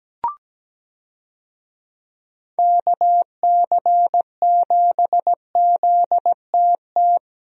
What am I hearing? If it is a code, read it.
Morse code, heard as a keyed tone, about 17 wpm, 700 Hz. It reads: KC7ZTT
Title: USA Call Signs - 6 Characters - Repeated 3x - 17wpm